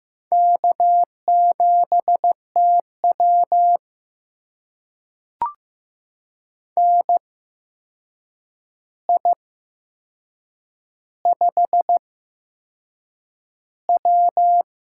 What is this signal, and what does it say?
Morse code, heard as a keyed tone, about 15 wpm, spaced as for 6 wpm, 700 Hz. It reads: K7TW NI5W
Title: USA Call Signs - 4 Characters - 8x Character Spacing - 15wpm